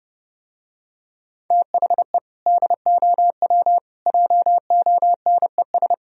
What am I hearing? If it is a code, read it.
Morse code, heard as a keyed tone, about 31 wpm, 700 Hz. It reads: THE DOW JONES